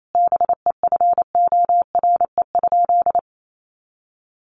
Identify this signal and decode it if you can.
Morse code, heard as a keyed tone, about 28 wpm, 700 Hz. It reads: BEFORE?
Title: Sentences from Top 900 Words 28wpm